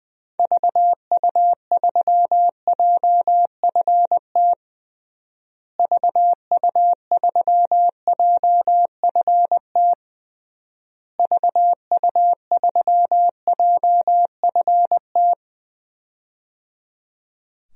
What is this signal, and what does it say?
Morse code, heard as a keyed tone, about 21 wpm, 700 Hz. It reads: VU3JFT VU3JFT VU3JFT
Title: DXCC-100 Call Signs - 6 Characters - Repeated 3x - 20wpm